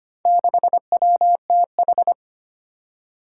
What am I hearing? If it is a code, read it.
Morse code, heard as a keyed tone, about 25 wpm, 700 Hz. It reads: 6WTH
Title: Four Challenging Letter-Number Combinations 25wpm